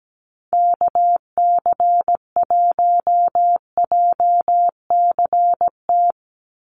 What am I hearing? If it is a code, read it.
Morse code, heard as a keyed tone, about 17 wpm, 700 Hz. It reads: KC1JCT
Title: USA Call Signs 17wpm